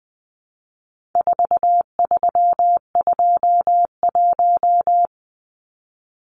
Morse code, heard as a keyed tone, about 20 wpm, 700 Hz. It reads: 4321
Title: Numbers - 4-Digit 20wpm